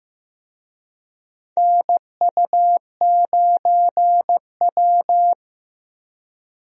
Morse code, heard as a keyed tone, about 15 wpm, 700 Hz. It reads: NU9W
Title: USA Call Signs - 4 Characters - Repeated 3x - 15wpm